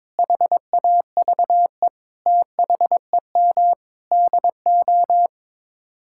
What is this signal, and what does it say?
Morse code, heard as a keyed tone, about 22 wpm, 700 Hz. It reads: HAVE THEM DO
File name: Sets of 9 Words - 22wpm